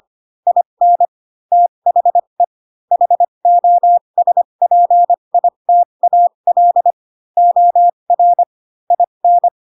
Morse code, heard as a keyed tone, about 25 wpm, spaced as for 22 wpm, 700 Hz. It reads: IN THE HOSPITAL OR IN
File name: Pinocchio / Pinocchio ch4 18 25 ik1hge